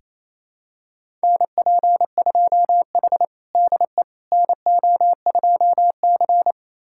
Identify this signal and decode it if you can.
Morse code, heard as a keyed tone, about 28 wpm, 700 Hz. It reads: NP2H DE NO2C
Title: USA Callsign DE Callsign - 4 Characters - 28wpm